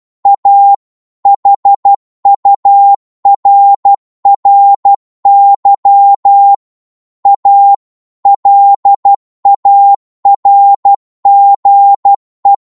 Morse code, heard as a keyed tone, about 12 wpm, 800 Hz. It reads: A HURRY A LARGE